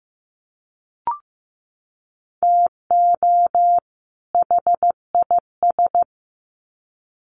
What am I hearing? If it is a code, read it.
Morse code, heard as a keyed tone, about 15 wpm, 700 Hz. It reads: TO HIS
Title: Sets of 2 Words 15wpm